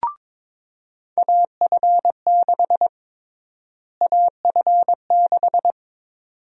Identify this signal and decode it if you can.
Morse code, heard as a keyed tone, about 22 wpm, 700 Hz. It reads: AF6 AF6
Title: USA Call Sign Prefixes - Repeated 2x - 22wpm